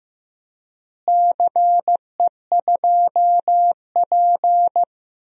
Morse code, heard as a keyed tone, about 15 wpm, 700 Hz. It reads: CE2P
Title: DXCC-100 Call Signs - 4 Characters - Repeated 3x - 15wpm